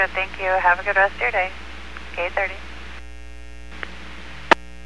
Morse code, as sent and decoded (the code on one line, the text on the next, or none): none